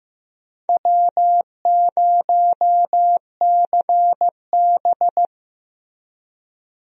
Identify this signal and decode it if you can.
Morse code, heard as a keyed tone, about 15 wpm, 700 Hz. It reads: W0CB
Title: USA Call Signs - 4 Characters - Repeated 3x - 15wpm